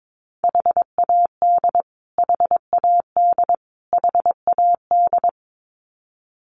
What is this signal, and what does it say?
Morse code, heard as a keyed tone, about 22 wpm, 700 Hz. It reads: HAD HAD HAD
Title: Top 300 Words Repeated 22wpm